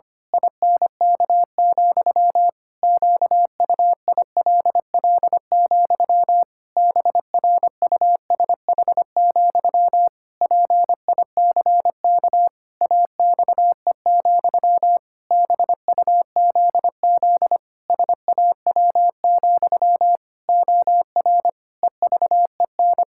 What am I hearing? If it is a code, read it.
Morse code, heard as a keyed tone, about 25 wpm, 700 Hz. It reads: INK, QUILL, BRUSH, PICK AXE, BUZZ SAW, OR EVEN
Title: Pangram Challenge v2 - 25wpm